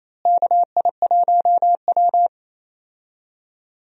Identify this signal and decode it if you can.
Morse code, heard as a keyed tone, about 28 wpm, 700 Hz. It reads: KI1W
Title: USA Call Signs - 4 Characters - Repeated 3x - 28wpm